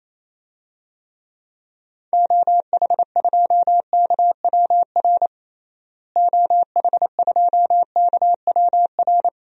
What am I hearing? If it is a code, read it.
Morse code, heard as a keyed tone, about 28 wpm, 700 Hz. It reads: OH2KWR OH2KWR
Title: DXCC-100 Call Signs - 6 Characters - Repeated 2x - 28wpm